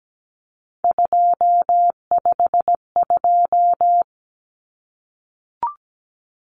Morse code, heard as a keyed tone, about 17 wpm, 700 Hz. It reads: 252
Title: Common Cut Numbers - 3-Digit 17wpm